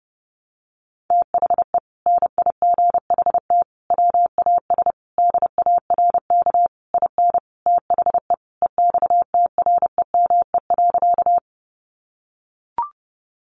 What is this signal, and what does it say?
Morse code, heard as a keyed tone, about 30 wpm, 700 Hz. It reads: THE NIGHT WAS DARK IN THE EXTREME.
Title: Sentences from Top 1300 Words 30wpm